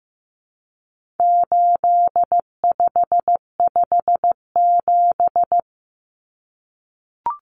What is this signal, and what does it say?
Morse code, heard as a keyed tone, about 15 wpm, 700 Hz. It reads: 8557
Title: Numbers - 4-Digit 15wpm